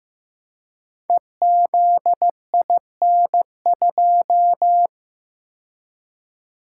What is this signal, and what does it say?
Morse code, heard as a keyed tone, about 15 wpm, 700 Hz. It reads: EZIN2
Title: Code Groups - 5 Letter:Numbers - 15wpm